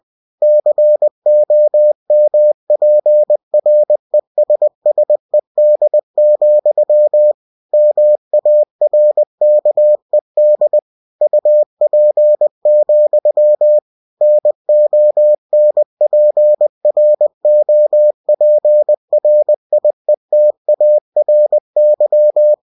Morse code, heard as a keyed tone, about 20 wpm, 600 Hz. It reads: COMPRESSED, MARKED UP, NONPROPRIETARY
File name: cw-books / winnie-the-pooh / winnie-20wpm-page-0116